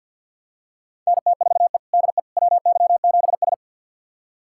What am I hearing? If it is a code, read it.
Morse code, heard as a keyed tone, about 52 wpm, 700 Hz. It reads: NT4E DE WY7S
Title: USA Callsign DE Callsign - 4 Characters - 50wpm